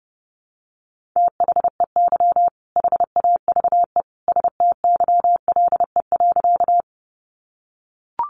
Morse code, heard as a keyed tone, about 30 wpm, 700 Hz. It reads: THEY HAVE STYLE.
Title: Sentences from Top 800 Words 30wpm